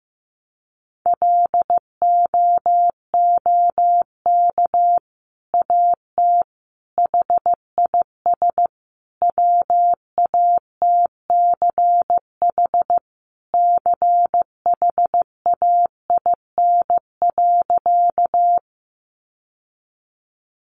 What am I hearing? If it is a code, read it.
Morse code, heard as a keyed tone, about 15 wpm, 700 Hz. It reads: LOOK AT HIS WATCH CHAIN.
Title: Sentences from Top 1500 Words 15wpm